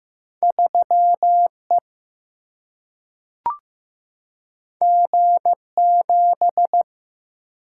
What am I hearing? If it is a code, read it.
Morse code, heard as a keyed tone, about 15 wpm, 700 Hz. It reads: 3 E G 7
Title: Single-Letter Single-Number 15wpm